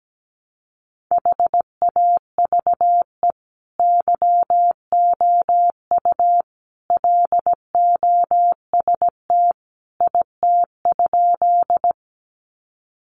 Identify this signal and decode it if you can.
Morse code, heard as a keyed tone, about 17 wpm, 700 Hz. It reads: HAVE YOU LOST IT?
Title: Sentences from Top 1400 Words 17wpm